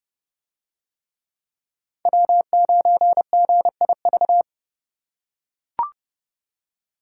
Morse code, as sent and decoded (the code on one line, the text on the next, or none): .-- ----. --. .. ...-
W9GIV